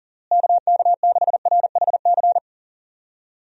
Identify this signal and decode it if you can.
Morse code, heard as a keyed tone, about 37 wpm, 700 Hz. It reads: KK6RHC
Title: USA Call Signs - 6 Characters - Repeated 3x - 40wpm